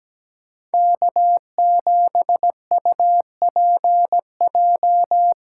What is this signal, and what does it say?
Morse code, heard as a keyed tone, about 17 wpm, 700 Hz. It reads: K7UPJ
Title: K1USN SST QSO Practice - 17wpm